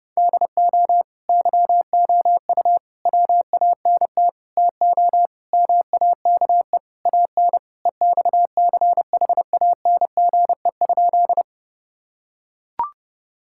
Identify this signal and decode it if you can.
Morse code, heard as a keyed tone, about 30 wpm, 700 Hz. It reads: DO YOU WANT TO MAKE AN EXCHANGE?
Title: Sentences from Top 1000 Words - Review Part 6 - 30wpm